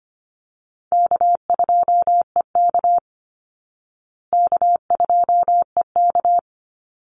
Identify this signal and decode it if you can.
Morse code, heard as a keyed tone, about 25 wpm, 700 Hz. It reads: K2EK K2EK
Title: Easy Call Signs Repeated 2x - 25wpm